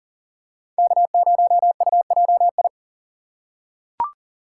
Morse code, heard as a keyed tone, about 40 wpm, 700 Hz. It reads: K0UJI
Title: USA Call Signs - 5 Characters - 2x Character Spacing - 40wpm